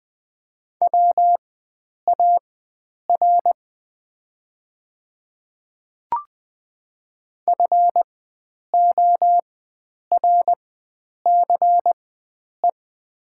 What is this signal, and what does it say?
Morse code, heard as a keyed tone, about 20 wpm, spaced as for 9 wpm, 690 Hz. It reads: WAR FORCE
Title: 4x Character Spacing - Top 500 Words - 20wpm